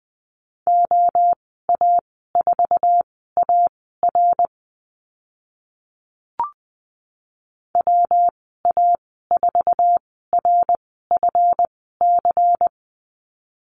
Morse code, heard as a keyed tone, about 20 wpm, spaced as for 14 wpm, 700 Hz. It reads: OA4AR WA4RFC
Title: 140 Lesson 23g - Letter f - Call Signs 20wpm